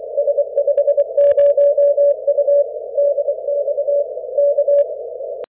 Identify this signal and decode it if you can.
Morse code, heard as a keyed tone, about 24 wpm, 570 Hz. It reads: S50U DX K
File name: S50U Dan 160M CW at K4SV